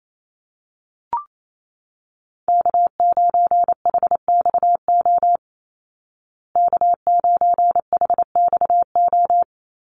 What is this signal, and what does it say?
Morse code, heard as a keyed tone, about 28 wpm, 700 Hz. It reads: K9HXO K9HXO
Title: Call Signs Repeated 2x - 28wpm